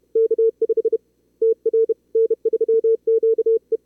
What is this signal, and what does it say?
Morse code, heard as a keyed tone, about 36 wpm, spaced as for 27 wpm, 440 Hz. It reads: K5TR N3QE
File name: CWSprint / Feb2022 / k5tr-feb-na-sprint-cw-k5ot -2300